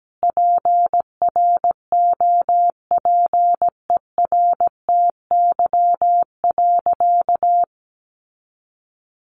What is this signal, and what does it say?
Morse code, heard as a keyed tone, about 17 wpm, 700 Hz. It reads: PROPERTY.
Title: Sentences from Top 1000 Words 17wpm